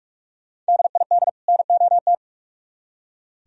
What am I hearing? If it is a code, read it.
Morse code, heard as a keyed tone, about 59 wpm, spaced as for 51 wpm, 700 Hz. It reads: DID NOT
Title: Speed-Racing - Sets of 2 Words 45wpm